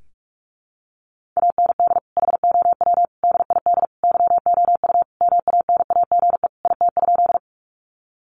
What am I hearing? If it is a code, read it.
Morse code, heard as a keyed tone, about 44 wpm, 700 Hz. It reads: AND HOW DID YOU MANAGE IT?
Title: Sentences from Top 1400 Words 45wpm